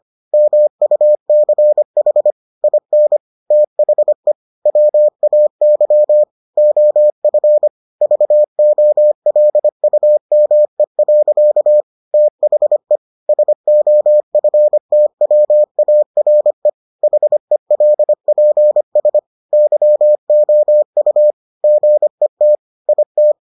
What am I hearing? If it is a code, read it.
Morse code, heard as a keyed tone, about 25 wpm, 600 Hz. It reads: MUCH IN THE WAY OF VOLUME. THE SOFTWARE HELPS YOU GET IT